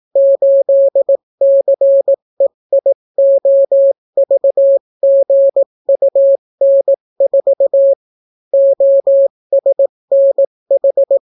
Morse code, heard as a keyed tone, about 18 wpm, 550 Hz. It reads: 8CEIOVGUN4 OSNH